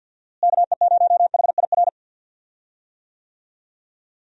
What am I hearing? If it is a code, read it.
Morse code, heard as a keyed tone, about 52 wpm, 700 Hz. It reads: KE0HIR
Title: USA Call Signs - 6 Characters - Repeated 3x - 50wpm